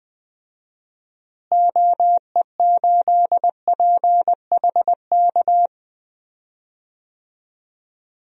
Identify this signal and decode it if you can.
Morse code, heard as a keyed tone, about 21 wpm, 700 Hz. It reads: OE8PHK
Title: DXCC-100 Call Signs - 6 Characters - Repeated 3x - 20wpm